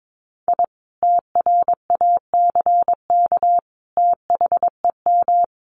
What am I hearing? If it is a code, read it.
Morse code, heard as a keyed tone, about 22 wpm, 700 Hz. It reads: I TRACK THEM
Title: A Sound Of Thunder 22wpm